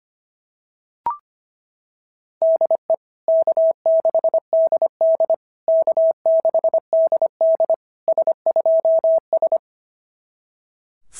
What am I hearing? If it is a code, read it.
Morse code, heard as a keyed tone, about 24 wpm, 650 Hz. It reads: DE K6DD K6DD S2S
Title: SOTA QSO Practice 25wpm